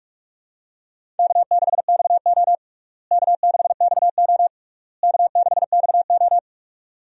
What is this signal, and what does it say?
Morse code, heard as a keyed tone, about 45 wpm, 700 Hz. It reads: K6XO K6XO K6XO
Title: USA Call Signs - 4 Characters - Repeated 3x - 45wpm